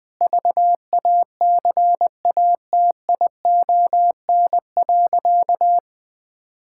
Morse code, heard as a keyed tone, about 19 wpm, 700 Hz. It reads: VACATION.
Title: Difficult Sentences with Hints - 20wpm